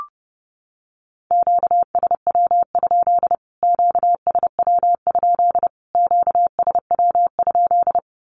Morse code, heard as a keyed tone, about 30 wpm, 700 Hz. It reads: QSW? QSW? QSW?
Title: Q-Codes Repeated 3x 30wpm